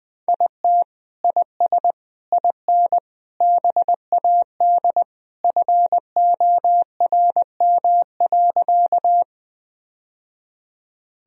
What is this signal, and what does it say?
Morse code, heard as a keyed tone, about 20 wpm, 700 Hz. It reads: IT IS IN BAD FORM.
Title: Sentences from Top 1000 Words - Review Part 1 - 20wpm